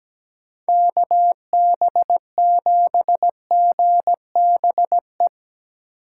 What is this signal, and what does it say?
Morse code, heard as a keyed tone, about 17 wpm, 700 Hz. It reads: KB7GBE
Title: USA Call Signs - 6 Characters - 4x Character Spacing - 17wpm